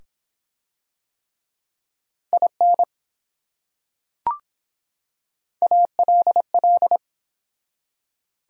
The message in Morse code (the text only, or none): IN ALL